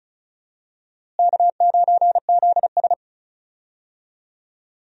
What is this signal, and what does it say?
Morse code, heard as a keyed tone, about 34 wpm, 700 Hz. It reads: K9ZS
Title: USA Call Signs - 4 Characters - Repeated 3x - 35wpm